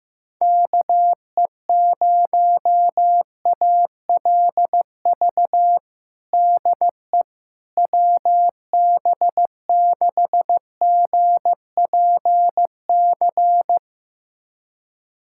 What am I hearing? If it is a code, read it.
Morse code, heard as a keyed tone, about 15 wpm, 700 Hz. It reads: KE0ALV DE WB6GPC
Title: USA Callsign DE Callsign - 6 Characters - 15wpm